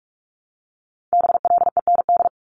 Morse code, heard as a keyed tone, about 46 wpm, 700 Hz. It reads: BLEND
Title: Multi-letter Phonemes 45wpm